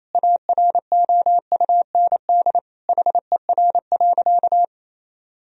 Morse code, heard as a keyed tone, about 28 wpm, 700 Hz. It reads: AROUND HER.
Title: Sentences from Top 1000 Words - Review Part 4 - 28wpm